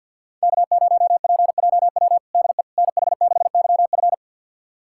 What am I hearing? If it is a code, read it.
Morse code, heard as a keyed tone, about 51 wpm, 700 Hz. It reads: K0PJW DE NH6YF
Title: USA Callsign DE Callsign - 5 Characters - 50wpm